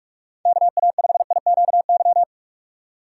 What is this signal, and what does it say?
Morse code, heard as a keyed tone, about 46 wpm, 700 Hz. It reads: KA5IQY
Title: USA Call Signs - 6 Characters - 2x Character Spacing - 45wpm